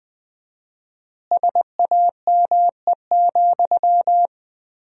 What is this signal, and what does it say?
Morse code, heard as a keyed tone, about 20 wpm, 700 Hz. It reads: SAME,
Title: 4x Word-Spacing - Sentences from Top 300 Words - 20wpm